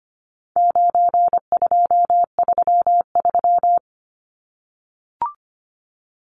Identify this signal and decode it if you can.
Morse code, heard as a keyed tone, about 25 wpm, 700 Hz. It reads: 9233
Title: Numbers - 4-Digit 25wpm